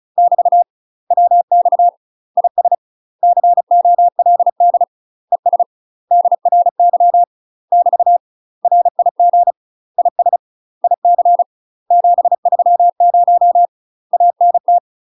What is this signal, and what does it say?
Morse code, heard as a keyed tone, about 35 wpm, 700 Hz. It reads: <BT> WX IS COLD ES DRY <BT> RIG IS IC 730 ANT
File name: UK MASTER QSOs TRAINING / 35WPM / UK MASTER QSOs 0043